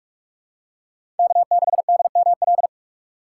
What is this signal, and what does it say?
Morse code, heard as a keyed tone, about 42 wpm, 700 Hz. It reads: K6DML
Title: USA Call Signs - 5 Characters - 6x Character Spacing - 45wpm